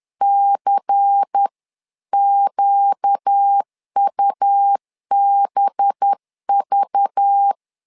Morse code, heard as a keyed tone, about 10 wpm, 800 Hz. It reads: CQUBV
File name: LEZIONI / 5 LBVC / 5 LBVC 10wpm 2349